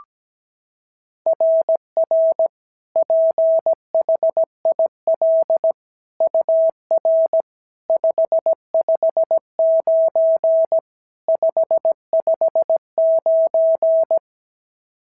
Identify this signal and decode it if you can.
Morse code, heard as a keyed tone, about 16 wpm, 650 Hz. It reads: RR PHIL UR 559 559
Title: SOTA QSO Practice 17wpm